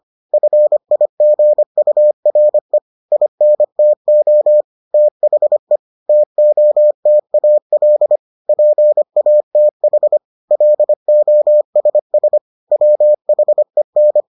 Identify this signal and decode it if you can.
Morse code, heard as a keyed tone, about 25 wpm, 600 Hz. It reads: FIGURE INTO THE TOTAL PATH LOSS WHEN